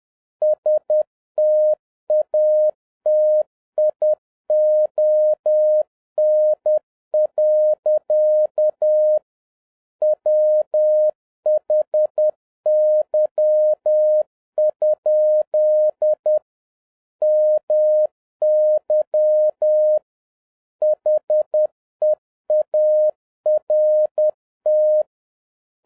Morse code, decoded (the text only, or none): STATION. WHY? MY HEART